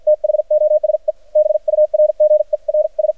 Morse code, heard as a keyed tone, about 53 wpm, spaced as for 44 wpm, 610 Hz. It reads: THOSE BURGERS
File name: archived / CRSnet